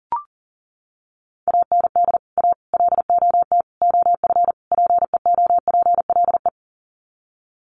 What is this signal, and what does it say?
Morse code, heard as a keyed tone, about 40 wpm, 700 Hz. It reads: AND A LOT OF PEOPLE
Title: Sets of 5 Words 40wpm